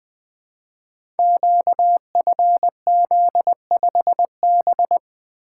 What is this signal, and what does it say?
Morse code, heard as a keyed tone, about 20 wpm, 700 Hz. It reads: QFZ5B
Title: Code Groups - 5 Letter:Numbers - 20wpm